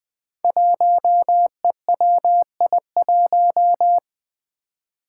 Morse code, heard as a keyed tone, about 20 wpm, 700 Hz. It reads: 1EWI1
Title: Five Challenging Letter-Number Combinations 20wpm